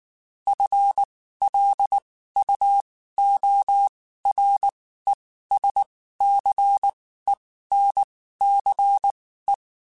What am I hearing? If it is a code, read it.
Morse code, heard as a keyed tone, about 19 wpm, spaced as for 14 wpm, 800 Hz. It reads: FLUORESCENCE
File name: Spaced / FL Runs sp mp3s / FL Runs sp @18